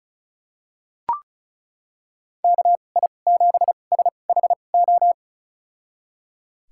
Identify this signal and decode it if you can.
Morse code, heard as a keyed tone, about 36 wpm, spaced as for 25 wpm, 700 Hz. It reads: KI7SHO
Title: USA Call Signs - 6 Characters - 2x Character Spacing - 35wpm